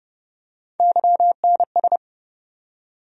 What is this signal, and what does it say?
Morse code, heard as a keyed tone, about 30 wpm, 700 Hz. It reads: YNS
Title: USA Call Sign Postfixes - 3 Letters 30wpm